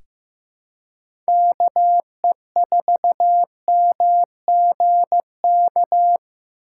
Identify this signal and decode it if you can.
Morse code, heard as a keyed tone, about 15 wpm, 700 Hz. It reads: KE4MGK
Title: USA Call Signs - 6 Characters - 2x Character Spacing - 15wpm